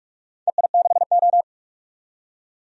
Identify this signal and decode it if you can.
Morse code, heard as a keyed tone, about 42 wpm, 700 Hz. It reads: EI6O
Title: DXCC-100 Call Signs - 4 Characters - 45wpm